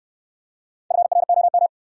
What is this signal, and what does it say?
Morse code, heard as a keyed tone, about 65 wpm, 700 Hz. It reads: FROM